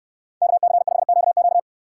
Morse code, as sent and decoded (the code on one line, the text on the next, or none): -.- -.-. ..... --.- --.-
KC5QQ